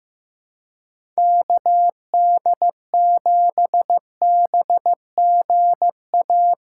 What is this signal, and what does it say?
Morse code, heard as a keyed tone, about 15 wpm, 700 Hz. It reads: KD7BGA
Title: USA Call Signs - 6 Characters - 4x Character Spacing - 15wpm